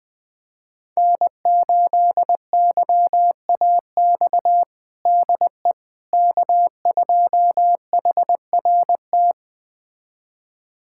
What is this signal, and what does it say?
Morse code, heard as a keyed tone, about 21 wpm, 700 Hz. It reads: N8YAX DE K2HRT
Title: USA Callsign DE Callsign - 5 Characters - 20wpm